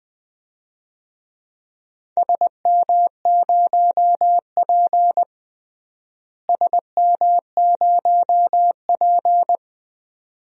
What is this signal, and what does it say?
Morse code, heard as a keyed tone, about 20 wpm, 700 Hz. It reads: SM0P SM0P
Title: DXCC-100 Call Signs - 4 Characters - Repeated 2x - 20wpm